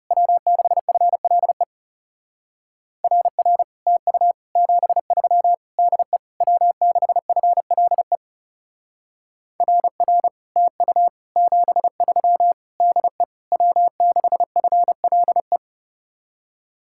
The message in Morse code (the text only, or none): W6FLE RR TU 73 DE W6FLE RR TU 73 DE W6FLE